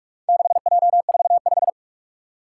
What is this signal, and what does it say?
Morse code, heard as a keyed tone, about 41 wpm, 700 Hz. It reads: 6J45
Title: Four Challenging Letter-Number Combinations 45wpm